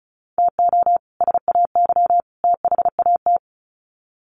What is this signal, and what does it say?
Morse code, heard as a keyed tone, about 34 wpm, 700 Hz. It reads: TO SAY THAT
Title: Sets of 3 Words 35wpm